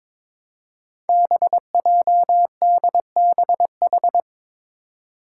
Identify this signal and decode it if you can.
Morse code, heard as a keyed tone, about 22 wpm, 700 Hz. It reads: BJDBH
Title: Five Challenging Letter-Number Combinations 22wpm